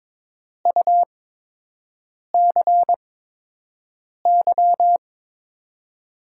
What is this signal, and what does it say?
Morse code, heard as a keyed tone, about 21 wpm, spaced as for 6 wpm, 700 Hz. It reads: UCY
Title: USA Call Signs - 5 Characters - 8x Character Spacing - 22wpm